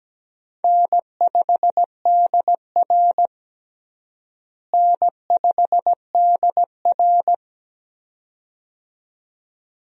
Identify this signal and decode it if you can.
Morse code, heard as a keyed tone, about 17 wpm, 700 Hz. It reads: N5DR N5DR
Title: USA Call Signs - 4 Characters - Repeated 2x - 17wpm